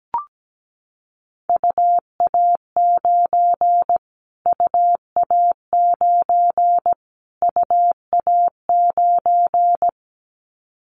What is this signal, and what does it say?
Morse code, heard as a keyed tone, about 17 wpm, 700 Hz. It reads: UA9 UA9 UA9
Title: Callsign Prefixes for Top 25 DXCC - Repeated 3x - 17wpm